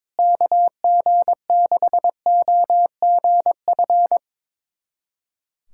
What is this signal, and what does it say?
Morse code, heard as a keyed tone, about 21 wpm, 700 Hz. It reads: KG6OGF
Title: USA Callsign DE Callsign - 6 Characters - 22wpm